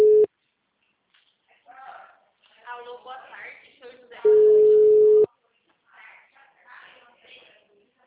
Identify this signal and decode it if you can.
Ringback tone (Europe, 425 Hz), with speech.